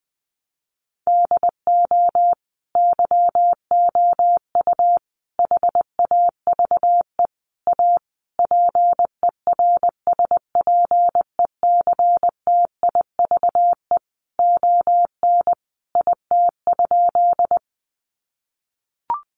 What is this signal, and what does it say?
Morse code, heard as a keyed tone, about 20 wpm, 700 Hz. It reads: DO YOU HAVE A PERSPECTIVE ON IT?
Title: Sentences from Top 1200 Words 20wpm